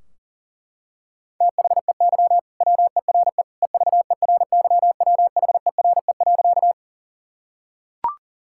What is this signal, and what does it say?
Morse code, heard as a keyed tone, about 37 wpm, 700 Hz. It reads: THEY WERE EVERYWHERE.
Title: Sentences from Top 1600 Words 40wpm